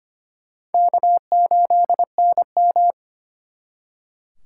Morse code, heard as a keyed tone, about 30 wpm, 700 Hz. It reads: K8NM
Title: Speed Racing - USA 1x2 Call Signs 25wpm